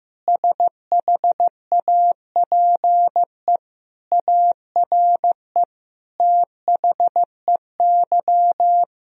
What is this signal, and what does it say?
Morse code, heard as a keyed tone, about 15 wpm, 700 Hz. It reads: SHAPE ARE THEY
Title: Sentences from Top 1000 Words - Review Part 3 - 15wpm